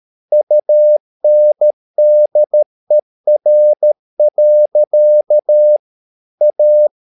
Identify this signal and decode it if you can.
Morse code, heard as a keyed tone, about 13 wpm, 600 Hz. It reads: UNDER. A
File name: mp3 files / Lexiphiles - 13wpm